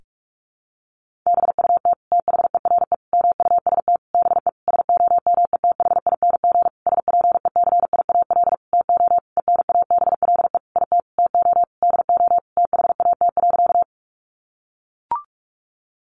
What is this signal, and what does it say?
Morse code, heard as a keyed tone, about 44 wpm, 700 Hz. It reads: BUT THERE MUST BE SOMETHING SPECIAL TO ENABLE IT TO DO THAT.
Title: Sentences from Top 1500 Words 45wpm